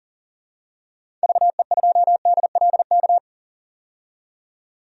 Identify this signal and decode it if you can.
Morse code, heard as a keyed tone, about 40 wpm, 700 Hz. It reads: VE2DLK
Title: DXCC-100 Call Signs - 6 Characters - 40wpm